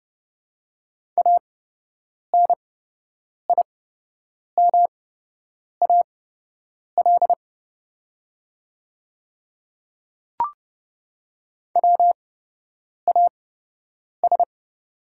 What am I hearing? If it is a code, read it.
Morse code, heard as a keyed tone, about 29 wpm, spaced as for 8 wpm, 690 Hz. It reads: ANIMAL WAS
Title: 8x Character Spacing - Top 200 Words - 30wpm